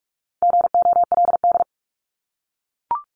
Morse code, heard as a keyed tone, about 44 wpm, 700 Hz. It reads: GOLD
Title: Top 500 Words 45wpm